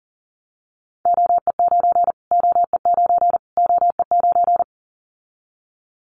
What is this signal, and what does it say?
Morse code, heard as a keyed tone, about 40 wpm, 700 Hz. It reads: OE9 OE9 OE9
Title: Callsign Prefixes for Top 25 DXCC - Repeated 3x - 40wpm